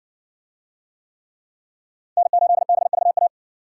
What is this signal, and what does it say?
Morse code, heard as a keyed tone, about 60 wpm, 700 Hz. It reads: N8BVA